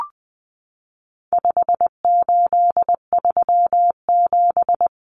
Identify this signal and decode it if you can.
Morse code, heard as a keyed tone, about 20 wpm, 700 Hz. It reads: 5837
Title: Numbers - 4-Digit 20wpm